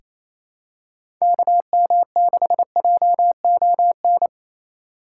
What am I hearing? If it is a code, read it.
Morse code, heard as a keyed tone, about 28 wpm, 700 Hz. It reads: KM6JON
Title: USA Call Signs - 6 Characters - 2x Character Spacing - 28wpm